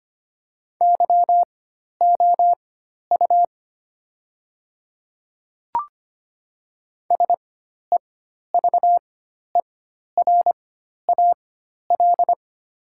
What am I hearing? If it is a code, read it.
Morse code, heard as a keyed tone, about 25 wpm, spaced as for 12 wpm, 700 Hz. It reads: YOU SEVERAL
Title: 4x Character Spacing - Top 500 Words - 25wpm